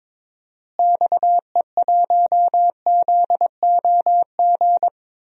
Morse code, heard as a keyed tone, about 21 wpm, 700 Hz. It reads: XE1ZOG
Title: DXCC-100 Call Signs - 6 Characters - Repeated 3x - 22wpm